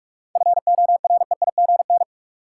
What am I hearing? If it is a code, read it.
Morse code, heard as a keyed tone, about 46 wpm, 700 Hz. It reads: FOREIGN
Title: Sentences from Top 1400 Words - New Word Drill - 45wpm